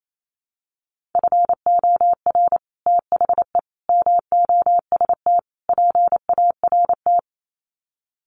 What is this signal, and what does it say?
Morse code, heard as a keyed tone, about 28 wpm, 700 Hz. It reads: FOR THE MOST PART
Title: Sets of 4 Words 28wpm